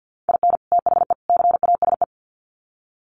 Morse code, heard as a keyed tone, about 50 wpm, 700 Hz. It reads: IN THE CASE
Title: Sets of 3 Words 50wpm